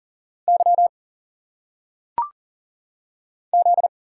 Morse code, heard as a keyed tone, about 40 wpm, 700 Hz. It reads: Y Z
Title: Single Letters - Phonetic Pronunciation 40wpm